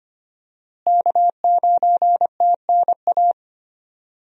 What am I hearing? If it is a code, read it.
Morse code, heard as a keyed tone, about 25 wpm, 700 Hz. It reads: K9TNA
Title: USA Call Signs - 5 Characters - Repeated 3x - 25wpm